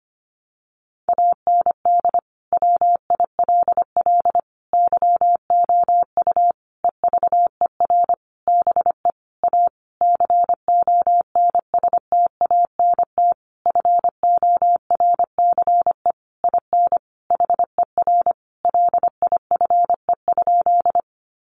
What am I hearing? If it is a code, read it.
Morse code, heard as a keyed tone, about 25 wpm, 700 Hz. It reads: AND WILL YOU EVER BE A CONSTANT FORCE IN HER LIFE?
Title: Sentences from Top 1500 Words 25wpm